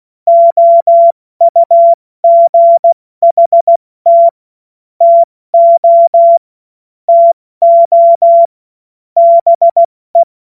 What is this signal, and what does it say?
Morse code, heard as a keyed tone, about 16 wpm, 680 Hz. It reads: OUGHT TO TO BE
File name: Morsekurs / Witze / Witz 27. September 2022 70 F80